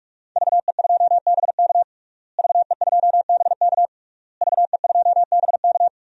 Morse code, heard as a keyed tone, about 41 wpm, 700 Hz. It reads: VE2BK VE2BK VE2BK
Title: DXCC-100 Call Signs - 5 Characters - Repeated 3x - 45wpm